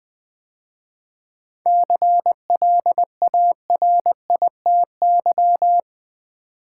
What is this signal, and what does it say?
Morse code, heard as a keyed tone, about 20 wpm, 700 Hz. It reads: CLARITY
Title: 7 Letter Words - First 3 Letters Given - 20wpm